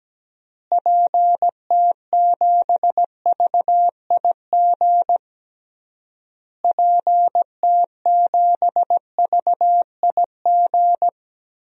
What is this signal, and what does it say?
Morse code, heard as a keyed tone, about 17 wpm, 700 Hz. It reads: PT7VIG PT7VIG
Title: DXCC-100 Call Signs - 6 Characters - Repeated 3x - 17wpm